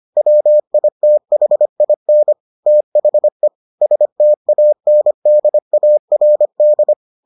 Morse code, heard as a keyed tone, about 25 wpm, 600 Hz. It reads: WITHIN THE STANDARD